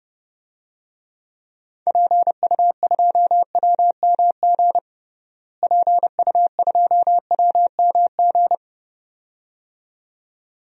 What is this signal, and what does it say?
Morse code, heard as a keyed tone, about 30 wpm, 700 Hz. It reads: PU2WMG PU2WMG
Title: DXCC-100 Call Signs - 6 Characters - Repeated 2x - 30wpm